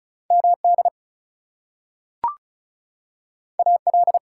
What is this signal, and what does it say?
Morse code, heard as a keyed tone, about 36 wpm, 700 Hz. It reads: MD AL
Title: US States and Canadian Provinces 35wpm